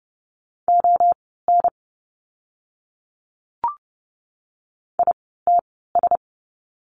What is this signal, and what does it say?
Morse code, heard as a keyed tone, about 30 wpm, spaced as for 17 wpm, 700 Hz. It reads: ON ITS
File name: course / 30:10wpm / 237 Lesson 38f - Letter q - Words 30wpm